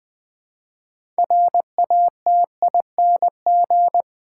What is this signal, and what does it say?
Morse code, heard as a keyed tone, about 20 wpm, 700 Hz. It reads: RATING
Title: 6 Letter Words - First 3 Letters Given - 20wpm